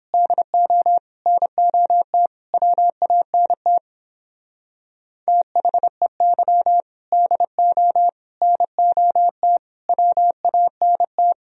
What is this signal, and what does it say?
Morse code, heard as a keyed tone, about 29 wpm, 700 Hz. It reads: DO NOT WANT THEY DO NOT WANT